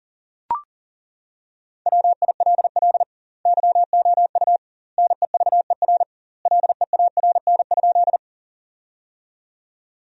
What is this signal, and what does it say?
Morse code, heard as a keyed tone, about 40 wpm, 700 Hz. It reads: WILL YOU NEVER LEARN?
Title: 2x Word-Spacing - Sentences from Top 300 Words - 40wpm